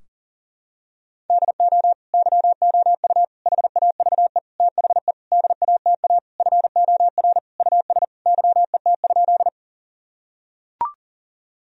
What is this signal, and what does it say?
Morse code, heard as a keyed tone, about 40 wpm, 700 Hz. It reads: DO YOU HAVE THE DATA FOR US YET?
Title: Sentences from Top 1000 Words - Review Part 3 - 40wpm